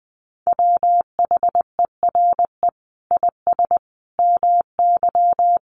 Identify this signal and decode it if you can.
Morse code, heard as a keyed tone, about 20 wpm, 700 Hz. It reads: WHERE IS MY
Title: Sentences from Top 1400 Words 20wpm